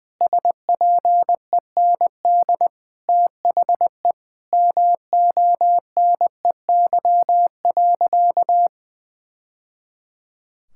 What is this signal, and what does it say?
Morse code, heard as a keyed tone, about 21 wpm, 700 Hz. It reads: SPEND THE MONEY.
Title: Sentences from Top 1000 Words - Review Part 5 - 20wpm